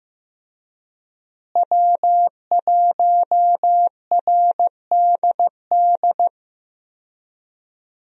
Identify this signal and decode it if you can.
Morse code, heard as a keyed tone, about 15 wpm, 700 Hz. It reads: W1RDD
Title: USA Call Signs - 5 Characters - Repeated 3x - 15wpm